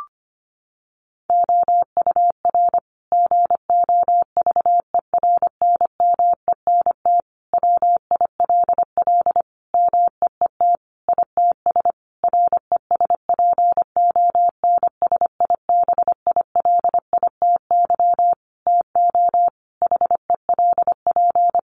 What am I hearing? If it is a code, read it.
Morse code, heard as a keyed tone, about 25 wpm, 700 Hz. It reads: OUR GOVERNMENT WILL MEET ITS RESPONSIBILITY TO HELP
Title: Sentences from Top 1200 Words 25wpm